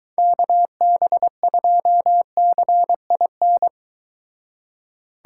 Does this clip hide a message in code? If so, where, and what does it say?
Yes: Morse code from 0.2 to 3.7 s, heard as a keyed tone, about 23 wpm, 700 Hz. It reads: KB2CIN